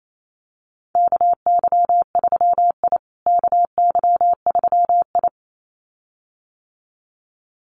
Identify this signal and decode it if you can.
Morse code, heard as a keyed tone, about 28 wpm, 700 Hz. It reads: KY3I KY3I
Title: USA Call Signs Repeated 2x 28wpm